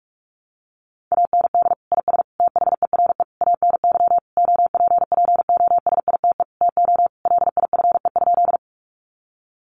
Morse code, heard as a keyed tone, about 44 wpm, 700 Hz. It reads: AND IS THERE ANY OPPOSITE TO LIFE?
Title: Sentences from Top 1300 Words 45wpm